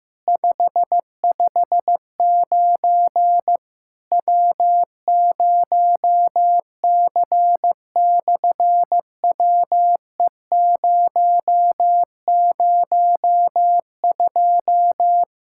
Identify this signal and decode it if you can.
Morse code, heard as a keyed tone, about 16 wpm, 700 Hz. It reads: 559 W0C/WE002
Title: SOTA QSO Practice 15wpm